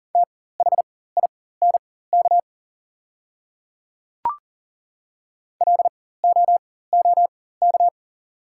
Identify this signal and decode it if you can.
Morse code, heard as a keyed tone, about 40 wpm, spaced as for 19 wpm, 690 Hz. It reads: THINK LOOK
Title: 4x Character Spacing - Top 300 Words - 40wpm